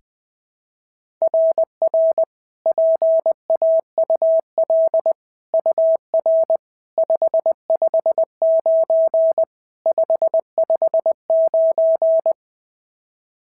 Morse code, heard as a keyed tone, about 18 wpm, 660 Hz. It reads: RR PAUL UR 559 559
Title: SOTA QSO Practice 20wpm